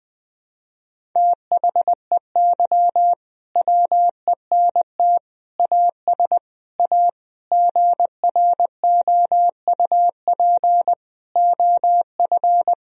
Morse code, heard as a keyed tone, about 20 wpm, 700 Hz. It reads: THEY WENT AS A GROUP OF
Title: Sentences from Top 1000 Words - Review Part 1 - 20wpm